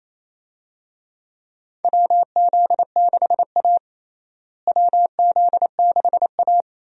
Morse code, heard as a keyed tone, about 28 wpm, 700 Hz. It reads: WZ6A WZ6A
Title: USA Call Signs - 4 Characters - Repeated 2x - 28wpm